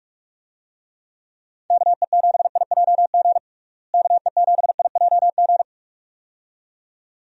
Morse code, heard as a keyed tone, about 43 wpm, 700 Hz. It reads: KE7IJG KE7IJG
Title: USA Call Signs - 6 Characters - Repeated 2x - 45wpm